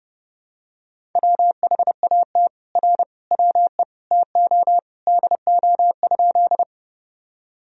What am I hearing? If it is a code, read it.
Morse code, heard as a keyed tone, about 30 wpm, 700 Hz. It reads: WHAT R WE TO DO?
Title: Sentences from Top 100 Words and Abbreviations 30wpm